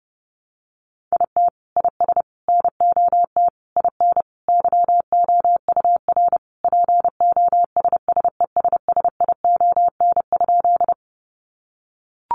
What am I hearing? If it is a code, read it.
Morse code, heard as a keyed tone, about 28 wpm, 700 Hz. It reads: IT IS NOT IN YOUR POSSESSION?
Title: Sentences from Top 1400 Words 30wpm